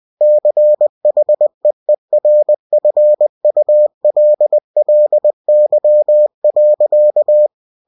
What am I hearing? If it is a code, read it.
Morse code, heard as a keyed tone, about 20 wpm, 600 Hz. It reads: CHEERFULLY.
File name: cw-books / winnie-the-pooh / winnie-20wpm-page-0019